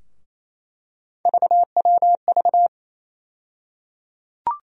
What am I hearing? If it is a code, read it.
Morse code, heard as a keyed tone, about 28 wpm, 700 Hz. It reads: VWV
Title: USA Call Sign Postfixes - 3 Letters 28wpm